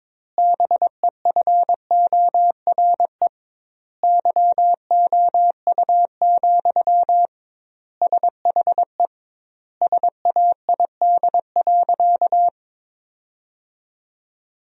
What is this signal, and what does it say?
Morse code, heard as a keyed tone, about 22 wpm, 700 Hz. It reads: BEFORE YOU, SHE SAID.
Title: 2x Word-Spacing - Sentences from Top 400 Words - 22wpm